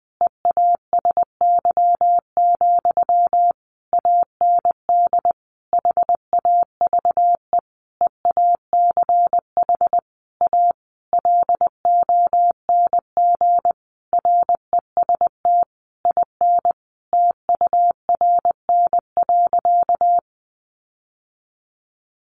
Morse code, heard as a keyed tone, about 20 wpm, 700 Hz. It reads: EASY, AND HAVE EACH A LONG REST IN TURN.
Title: Sentences from Top 700 Words 20wpm